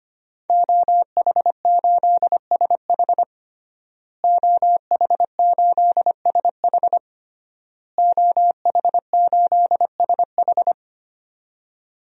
Morse code, heard as a keyed tone, about 25 wpm, 700 Hz. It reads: OH8SH OH8SH OH8SH
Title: DXCC-100 Call Signs - 5 Characters - Repeated 3x - 25wpm